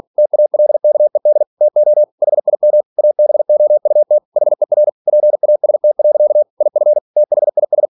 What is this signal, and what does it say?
Morse code, heard as a keyed tone, about 46 wpm, 600 Hz. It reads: TALKED TO HIM ABOUT HER PAST. IF THIS
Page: Never 47wpm